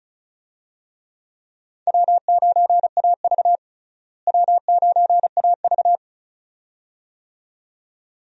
Morse code, heard as a keyed tone, about 34 wpm, 700 Hz. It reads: W9AV W9AV
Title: USA Call Signs - 4 Characters - Repeated 2x - 35wpm